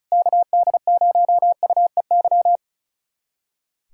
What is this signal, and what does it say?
Morse code, heard as a keyed tone, about 36 wpm, 700 Hz. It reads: KD0UEY
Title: USA Callsign DE Callsign 35wpm